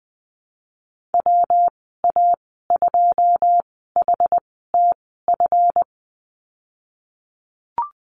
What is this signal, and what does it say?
Morse code, heard as a keyed tone, about 20 wpm, spaced as for 15 wpm, 700 Hz. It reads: WA2HTF
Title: 173 Lesson 28g - Number 9 - Call Signs 20wpm